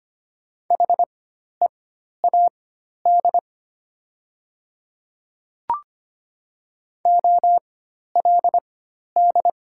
Morse code, heard as a keyed tone, about 25 wpm, spaced as for 12 wpm, 700 Hz. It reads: HEAD OLD
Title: 4x Character Spacing - Top 300 Words - 25wpm